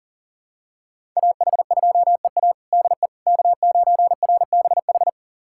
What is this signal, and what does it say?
Morse code, heard as a keyed tone, about 40 wpm, 700 Hz. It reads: AH2EA DE K9RBH
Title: USA Callsign DE Callsign - 5 Characters - 40wpm